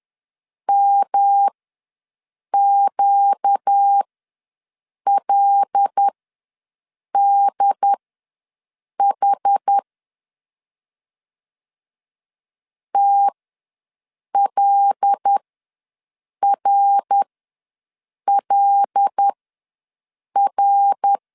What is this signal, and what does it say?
Morse code, heard as a keyed tone, about 10 wpm, spaced as for 6 wpm, 800 Hz. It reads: MQLDH TLRLR
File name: LEZIONI / 5 LBVC / 5 L ws 10wpm 3636